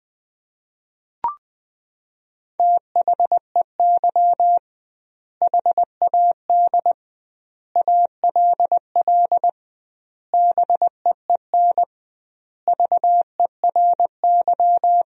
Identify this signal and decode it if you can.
Morse code, heard as a keyed tone, about 20 wpm, 700 Hz. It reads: THEY HAD ALL BEEN VERY
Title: 2x Word-Spacing - Sentences from Top 200 Words - 20wpm